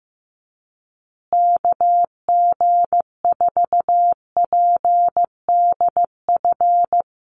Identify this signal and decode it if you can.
Morse code, heard as a keyed tone, about 15 wpm, 700 Hz. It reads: KG4PDF
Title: USA Call Signs Repeated 3x 15wpm